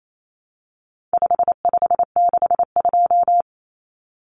Morse code, heard as a keyed tone, about 28 wpm, 700 Hz. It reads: 5562
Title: Common Cut Numbers - 4-Digit 28wpm